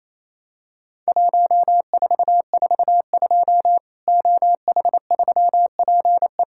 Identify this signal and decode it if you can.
Morse code, heard as a keyed tone, about 27 wpm, 700 Hz. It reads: 1442 OH3PE
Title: Worked All Europe DX Contest - QTC Exchange - 28wpm